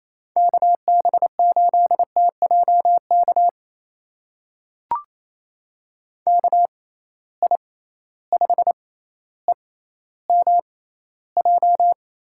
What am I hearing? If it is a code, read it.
Morse code, heard as a keyed tone, about 28 wpm, spaced as for 14 wpm, 700 Hz. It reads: KB8TJK KI5EMJ
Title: USA Call Signs - 6 Characters - 6x Character Spacing - 28wpm